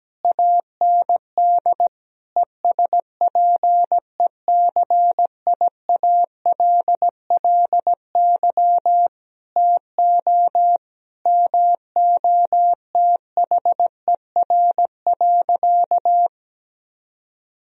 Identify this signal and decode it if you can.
Morse code, heard as a keyed tone, about 17 wpm, 700 Hz. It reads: AND ESPECIALLY TO MOTHER.
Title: Sentences from Top 1000 Words - Review Part 4 - 17wpm